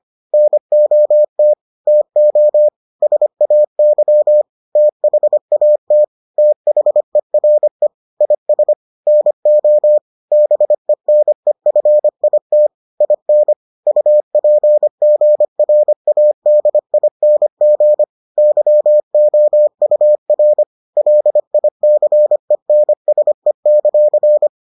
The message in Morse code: NOT TO SAY THAT THERE IS NO BENEFIT IN UPGRADING YOUR LICENSE;

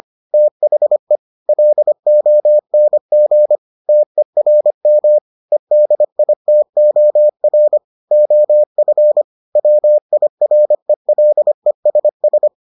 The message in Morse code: THE LONG TERM EDITOR OF WIRELESS

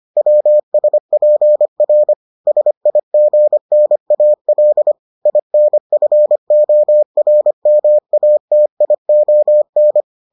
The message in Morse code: WSPR SIGNAL INFORMATION

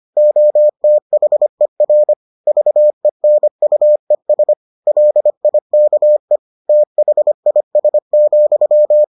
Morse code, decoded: OTHER VENUES LIKE THIS,

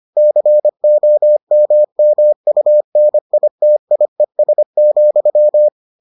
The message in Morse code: COMMUNITIES,

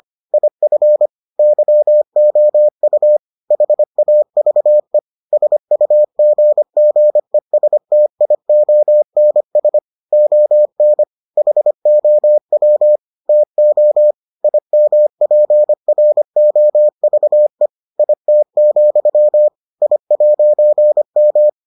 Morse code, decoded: IF YOU HAVE SUGGESTIONS ON HOW TO IMPROVE IT, I'M

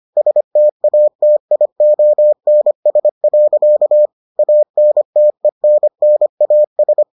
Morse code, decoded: STATIONS. ANTENNAS

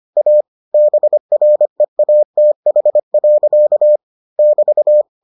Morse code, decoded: A BREATH. <BT>

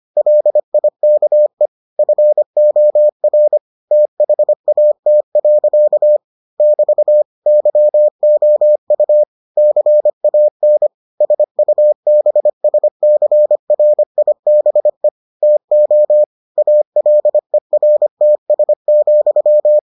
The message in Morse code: LIKE FOR THAT. <BT> YOU CAN SUBSCRIBE TO ALERTS,

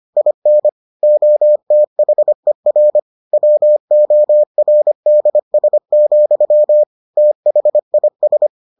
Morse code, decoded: IN OTHER WORDS, THIS

